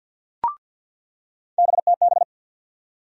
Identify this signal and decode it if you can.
Morse code, heard as a keyed tone, about 47 wpm, 700 Hz. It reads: BTB
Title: USA Call Sign Postfixes - 3 Letters 50wpm